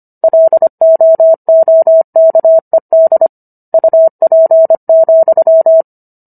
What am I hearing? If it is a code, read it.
Morse code, heard as a keyed tone, about 25 wpm, 650 Hz. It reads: LOOKED UP,